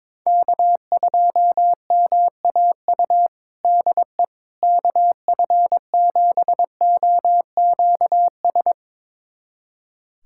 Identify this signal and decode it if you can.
Morse code, heard as a keyed tone, about 21 wpm, 700 Hz. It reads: K2MAU DE KF7OQS
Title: USA Callsign DE Callsign 22wpm